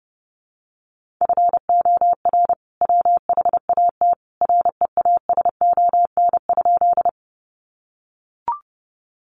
Morse code, heard as a keyed tone, about 30 wpm, 700 Hz. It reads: FOR WHAT REASON?
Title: Sentences from Top 700 Words 30wpm